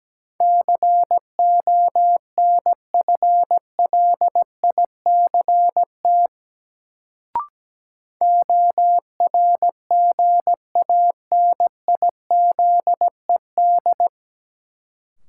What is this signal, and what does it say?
Morse code, heard as a keyed tone, about 17 wpm, 700 Hz. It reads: CONFLICT ORGANIZED
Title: Sentences from Top 1300 Words - New Word Drill - 17wpm